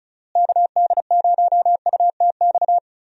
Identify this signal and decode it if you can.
Morse code, heard as a keyed tone, about 34 wpm, 700 Hz. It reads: KD0UTX
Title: USA Call Signs - 6 Characters - Repeated 2x - 35wpm